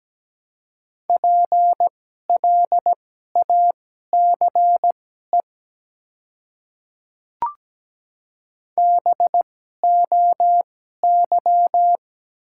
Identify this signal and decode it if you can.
Morse code, heard as a keyed tone, about 17 wpm, spaced as for 12 wpm, 700 Hz. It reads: PLACE BOY
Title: 2x Character Spacing - Top 200 Words - 17wpm